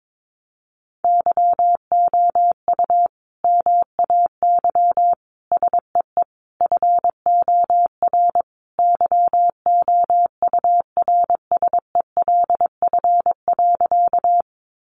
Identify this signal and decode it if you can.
Morse code, heard as a keyed tone, about 22 wpm, 700 Hz. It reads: YOU MAY SEE FOR YOURSELF.
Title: Sentences from Top 600 Words 22wpm